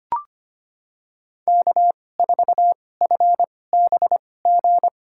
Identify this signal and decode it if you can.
Morse code, heard as a keyed tone, about 25 wpm, spaced as for 18 wpm, 700 Hz. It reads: K4FBG
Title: USA Call Signs - 5 Characters - 2x Character Spacing - 25wpm